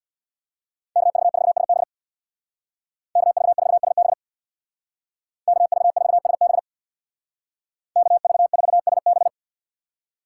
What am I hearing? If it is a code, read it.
Morse code, heard as a keyed tone, about 64 wpm, 700 Hz. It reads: KV4SB KV4SB KV4SB KV4SB